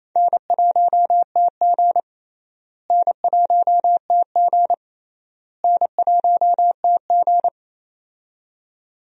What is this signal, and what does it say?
Morse code, heard as a keyed tone, about 28 wpm, 700 Hz. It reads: N1TG N1TG N1TG
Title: USA Call Signs - 4 Characters - Repeated 3x - 28wpm